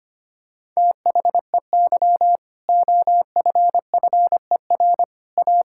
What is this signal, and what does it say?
Morse code, heard as a keyed tone, about 25 wpm, 700 Hz. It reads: THEY OFFER A
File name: practice / 25wpm / Sentences with 4-Word Phrases - 25wpm